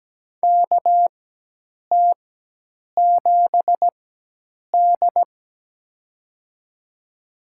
Morse code, heard as a keyed tone, about 17 wpm, spaced as for 8 wpm, 700 Hz. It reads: KT7D
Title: USA Call Signs - 4x Character Spacing - 17wpm